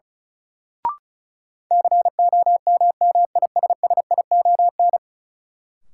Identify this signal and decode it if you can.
Morse code, heard as a keyed tone, about 36 wpm, 700 Hz. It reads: COMMISSION
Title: Sentences from Top 1400 Words - New Word Drill - 35wpm